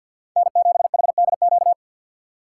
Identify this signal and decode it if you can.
Morse code, heard as a keyed tone, about 50 wpm, 700 Hz. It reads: N7HDQ
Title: USA Call Signs - 2x Character Spacing - 50wpm